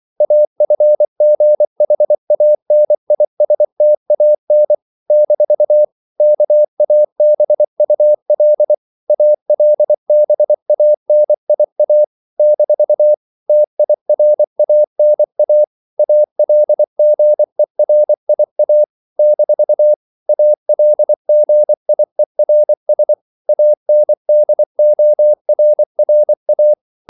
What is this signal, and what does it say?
Morse code, heard as a keyed tone, about 24 wpm, 600 Hz. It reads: AFGHANISTAN - KABUL ALBANIA - TIRANA ALGERIA - ALGIERS ANDORRA
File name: CW / Practice / NationalCapitals24